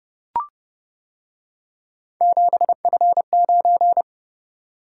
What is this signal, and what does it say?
Morse code, heard as a keyed tone, about 29 wpm, 700 Hz. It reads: 7F9